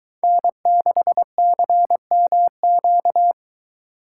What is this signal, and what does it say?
Morse code, heard as a keyed tone, about 22 wpm, 700 Hz. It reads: N6CMQ